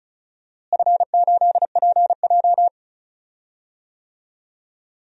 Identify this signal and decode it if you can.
Morse code, heard as a keyed tone, about 36 wpm, 700 Hz. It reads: F8PJ
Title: DXCC-100 Call Signs - 4 Characters - Repeated 2x - 35wpm